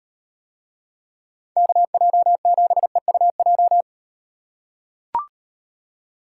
Morse code, heard as a keyed tone, about 38 wpm, 700 Hz. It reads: KJ7EUJ